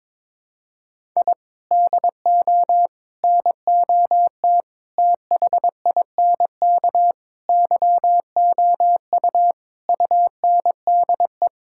Morse code, heard as a keyed tone, about 22 wpm, 700 Hz. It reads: I DO NOT THINK YOU UNDE
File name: practice / 22wpm / Sentences with 4-Word Phrases - 22wpm